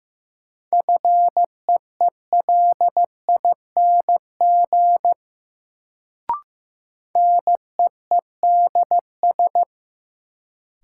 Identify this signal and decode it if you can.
Morse code, heard as a keyed tone, about 15 wpm, 700 Hz. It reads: FEELING NEEDS
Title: Sentences from Top 1000 Words - Review Part 3 - New Word Drill - 15wpm